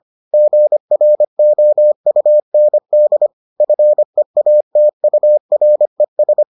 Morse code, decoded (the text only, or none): GROUND FEATURES